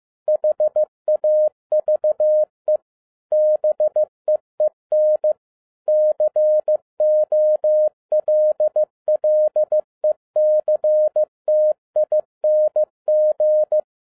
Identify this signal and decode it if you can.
Morse code, heard as a keyed tone, about 15 wpm, 610 Hz. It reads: HAVE BEEN COLLECTING
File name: 2011-07-mp3 / n22 QRP OTAKU 15wpm